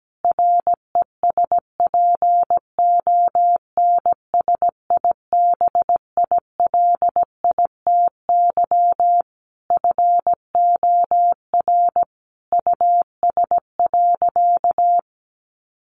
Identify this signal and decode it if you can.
Morse code, heard as a keyed tone, about 17 wpm, 700 Hz. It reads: RESPONSIBILITY FOR US.
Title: Sentences from Top 1500 Words 17wpm